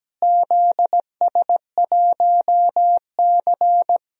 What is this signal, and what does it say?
Morse code, heard as a keyed tone, about 17 wpm, 700 Hz. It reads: ZS1C
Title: DXCC-100 Call Signs - 4 Characters - Repeated 3x - 17wpm